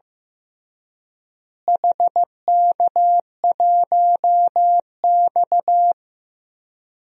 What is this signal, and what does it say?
Morse code, heard as a keyed tone, about 15 wpm, 700 Hz. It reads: HK1X
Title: DXCC-100 Call Signs - 4 Characters - Repeated 3x - 15wpm